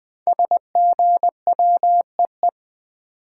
Morse code, heard as a keyed tone, about 20 wpm, 700 Hz. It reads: SGWEE
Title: Code Groups - 5 Letters - 20wpm